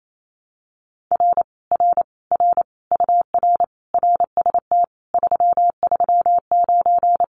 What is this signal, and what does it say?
Morse code, heard as a keyed tone, about 28 wpm, 700 Hz. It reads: R R R UR RST 339
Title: Common QSO Phrases Encore 28wpm